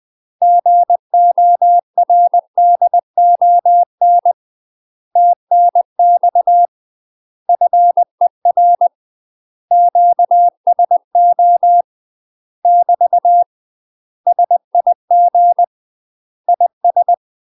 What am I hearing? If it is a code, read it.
Morse code, heard as a keyed tone, about 20 wpm, 700 Hz. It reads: GORDON TNX FER QSO <BT> SIG IS
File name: UK MASTER QSOs TRAINING / 20WPM / UK MASTER QSOs 0050